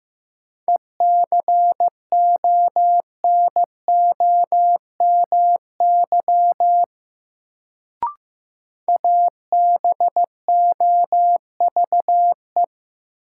Morse code, heard as a keyed tone, about 15 wpm, 700 Hz. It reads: ECONOMY ABOVE
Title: Sentences from Top 1200 Words - New Word Drill Alt - 15wpm